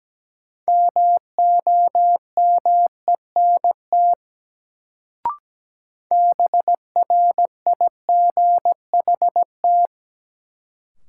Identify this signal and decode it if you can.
Morse code, heard as a keyed tone, about 17 wpm, 700 Hz. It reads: MOMENT BRIGHT
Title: Sentences from Top 1000 Words - New Word Drill - 17wpm